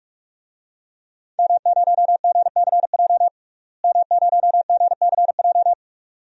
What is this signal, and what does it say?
Morse code, heard as a keyed tone, about 44 wpm, 700 Hz. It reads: M0GCJ M0GCJ
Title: DXCC-100 Call Signs - 5 Characters - Repeated 2x - 45wpm